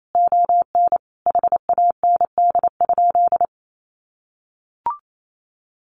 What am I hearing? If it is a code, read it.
Morse code, heard as a keyed tone, about 28 wpm, 700 Hz. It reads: ON HAND?
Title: Sentences from Top 1500 Words 28wpm